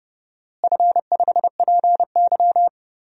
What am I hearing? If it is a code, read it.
Morse code, heard as a keyed tone, about 30 wpm, 700 Hz. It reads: F5PY
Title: DXCC-100 Call Signs - 4 Characters - Repeated 3x - 30wpm